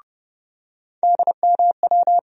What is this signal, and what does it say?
Morse code, heard as a keyed tone, about 30 wpm, 700 Hz. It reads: DMW
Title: USA Call Sign Postfixes - 3 Letters 30wpm